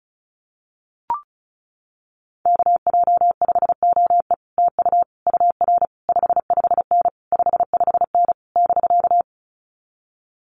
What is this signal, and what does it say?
Morse code, heard as a keyed tone, about 34 wpm, 700 Hz. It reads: KJ5OE TU UR 55N 55N <BK>
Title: Common QSO Phrases Encore 35wpm